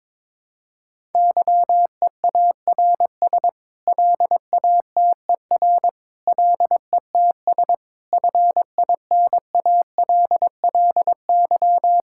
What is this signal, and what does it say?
Morse code, heard as a keyed tone, about 22 wpm, 700 Hz. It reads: YEARS LATER LETS FINALLY
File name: practice / 22wpm / Sets of 8 Words - 22wpm